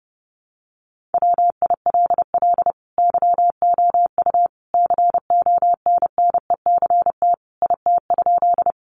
Morse code, heard as a keyed tone, about 30 wpm, 700 Hz. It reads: WILL YOU CONNECT IT?
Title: Sentences from Top 1500 Words 30wpm